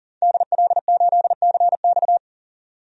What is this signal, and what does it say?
Morse code, heard as a keyed tone, about 40 wpm, 700 Hz. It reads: DL8CX
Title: DXCC-100 Call Signs - 5 Characters - 40wpm